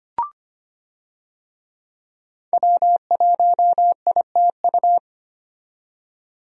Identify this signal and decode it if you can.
Morse code, heard as a keyed tone, about 25 wpm, 690 Hz. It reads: W1ITU
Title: USA Call Signs - 5 Characters - 25wpm